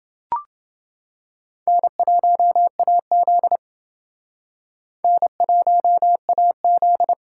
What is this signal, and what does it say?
Morse code, heard as a keyed tone, about 28 wpm, 700 Hz. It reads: N1AZ N1AZ